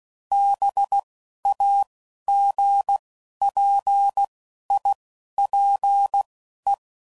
Morse code, heard as a keyed tone, about 16 wpm, spaced as for 11 wpm, 800 Hz. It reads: BAGPIPE